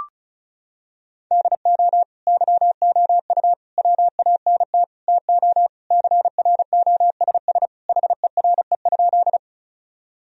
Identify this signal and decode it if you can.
Morse code, heard as a keyed tone, about 36 wpm, 700 Hz. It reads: DO YOU WANT TO CROSS HERE?
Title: Sentences from Top 1000 Words - Review Part 1 - 35wpm